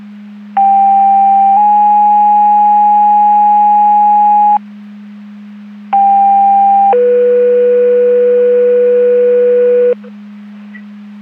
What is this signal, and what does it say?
Two-tone page, heard as a one-second tone, then a three-second tone, about 790 then 830 Hz, about 800 then 490 Hz. It is heard from 0.6 to 9.9 s.
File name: Portage 154.385 Paging